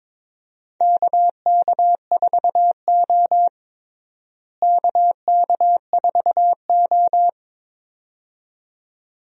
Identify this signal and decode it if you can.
Morse code, heard as a keyed tone, about 22 wpm, 700 Hz. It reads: KK4O KK4O
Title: USA Call Signs - 4 Characters - Repeated 2x - 22wpm